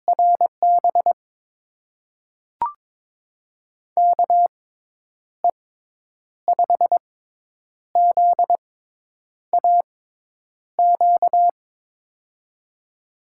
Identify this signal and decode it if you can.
Morse code, heard as a keyed tone, about 23 wpm, spaced as for 9 wpm, 700 Hz. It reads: RB KE5ZAQ
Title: USA Call Signs - 6x Character Spacing - 22wpm